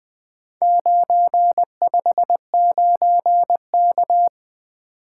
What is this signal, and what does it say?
Morse code, heard as a keyed tone, about 20 wpm, 700 Hz. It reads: 959K
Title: Four Letter-Number Combinations - 20wpm